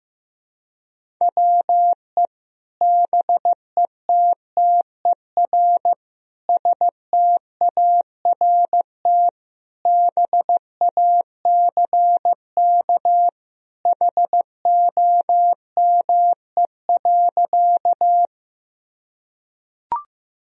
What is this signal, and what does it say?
Morse code, heard as a keyed tone, about 15 wpm, 700 Hz. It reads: WE BETTER START BACK HOME.
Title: Sentences from Top 1000 Words - Review Part 1 - 15wpm